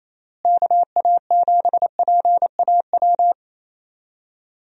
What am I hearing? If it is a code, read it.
Morse code, heard as a keyed tone, about 27 wpm, 700 Hz. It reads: KA7PAW
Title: USA Call Signs - 6 Characters - Repeated 3x - 28wpm